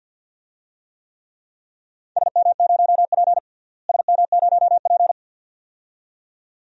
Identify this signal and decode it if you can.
Morse code, heard as a keyed tone, about 46 wpm, 700 Hz. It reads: SM0P SM0P
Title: DXCC-100 Call Signs - 4 Characters - Repeated 2x - 50wpm